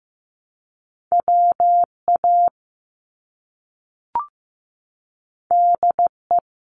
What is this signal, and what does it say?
Morse code, heard as a keyed tone, about 15 wpm, 700 Hz. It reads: WA DE
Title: US States 15wpm